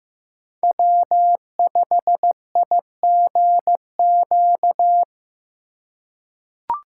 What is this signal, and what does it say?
Morse code, heard as a keyed tone, about 15 wpm, 700 Hz. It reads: W5IGQ
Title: USA Call Signs - 5 Characters - 4x Character Spacing - 15wpm